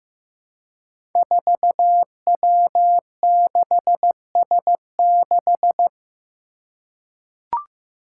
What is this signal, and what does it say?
Morse code, heard as a keyed tone, about 15 wpm, 700 Hz. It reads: 4W6S6
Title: Five Challenging Letter-Number Combinations 15wpm